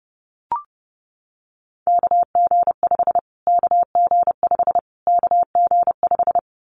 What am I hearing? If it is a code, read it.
Morse code, heard as a keyed tone, about 30 wpm, 700 Hz. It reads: KG5 KG5 KG5
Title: Callsign Prefixes for Top 25 DXCC - Repeated 3x - 30wpm